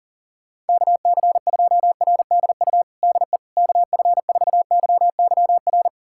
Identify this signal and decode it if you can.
Morse code, heard as a keyed tone, about 40 wpm, 700 Hz. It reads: KC2RDU DE KF4YYR
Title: USA Callsign DE Callsign 40wpm